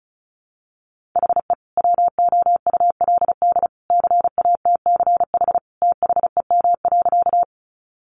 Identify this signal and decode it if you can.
Morse code, heard as a keyed tone, about 34 wpm, 700 Hz. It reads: HE WOULD CATCH THEM.
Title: Sentences from Top 1500 Words 35wpm